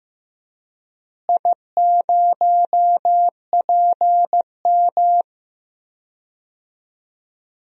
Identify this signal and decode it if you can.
Morse code, heard as a keyed tone, about 15 wpm, 700 Hz. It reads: I0PM
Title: DXCC-100 Call Signs - 4 Characters - Repeated 2x - 15wpm